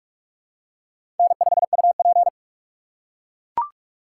Morse code, heard as a keyed tone, about 44 wpm, 700 Hz. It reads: N5UP
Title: USA Call Signs - 4 Characters - 2x Character Spacing - 45wpm